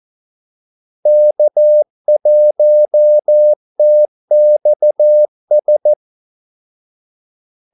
Morse code, heard as a keyed tone, about 14 wpm, 600 Hz. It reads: K1TXS
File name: Callsigns-14wpm